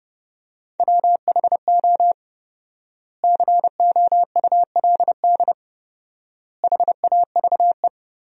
Morse code, heard as a keyed tone, about 30 wpm, 700 Hz. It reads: WHO COULD HAVE
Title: 4x Word-Spacing - Sentences from Top 300 Words - 30wpm